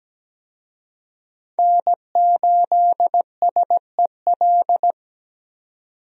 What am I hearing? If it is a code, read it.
Morse code, heard as a keyed tone, about 17 wpm, 700 Hz. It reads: N8SEL
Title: USA Call Signs - 5 Characters - 17wpm